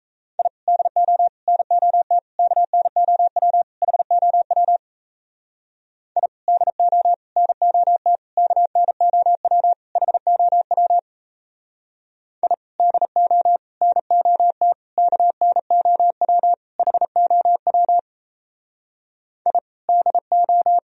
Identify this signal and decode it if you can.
Morse code, heard as a keyed tone, about 33 wpm, 700 Hz. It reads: I DO NOT KNOW HOW I DO NOT KNOW HOW I DO NOT KNOW HOW I DO
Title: Speed-Racing - Sets of 5 Words 28wpm